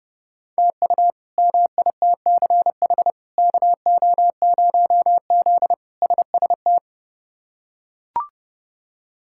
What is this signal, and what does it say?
Morse code, heard as a keyed tone, about 30 wpm, 700 Hz. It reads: TU MITCH KO0Z SST
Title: K1USN SST QSO Practice - 30wpm